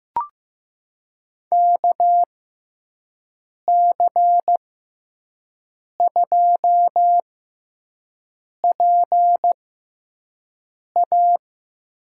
Morse code, heard as a keyed tone, about 15 wpm, spaced as for 5 wpm, 700 Hz. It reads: KC2PA
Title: USA Call Signs - 6 Characters - 6x Character Spacing - 15wpm